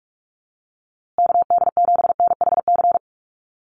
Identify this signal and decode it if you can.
Morse code, heard as a keyed tone, about 44 wpm, 700 Hz. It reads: KD7NHC
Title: USA Call Signs Repeated 3x 45wpm